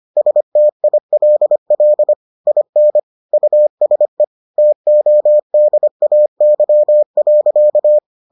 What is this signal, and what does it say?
Morse code, heard as a keyed tone, about 25 wpm, 600 Hz. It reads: STILL IN USE TODAY.